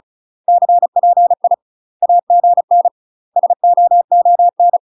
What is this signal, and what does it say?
Morse code, heard as a keyed tone, about 35 wpm, 700 Hz. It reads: CPI AGN SOON
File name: UK MASTER QSOs TRAINING / 35WPM / UK MASTER QSOs 0002